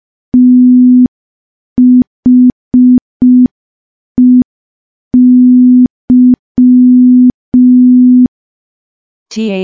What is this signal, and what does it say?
Morse code, heard as a keyed tone, about 5 wpm, 250 Hz. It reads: THEY
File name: morse / morse-training-05WPM